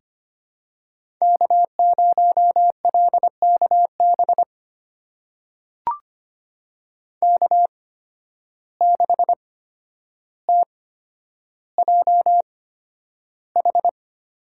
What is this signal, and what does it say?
Morse code, heard as a keyed tone, about 25 wpm, spaced as for 11 wpm, 700 Hz. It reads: K0LKB K6TJH
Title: USA Call Signs - 8x Character Spacing - 25wpm